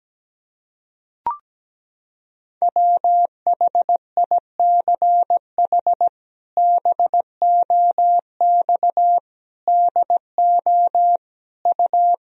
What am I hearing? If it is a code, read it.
Morse code, heard as a keyed tone, about 17 wpm, 700 Hz. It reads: WHICH BOX DO U
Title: Sentences from Top 500 Words and Abbreviations 17wpm